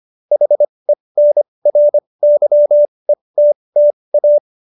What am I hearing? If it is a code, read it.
Morse code, heard as a keyed tone, about 25 wpm, spaced as for 20 wpm, 590 Hz. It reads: HENRYETTA
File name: QSO316 20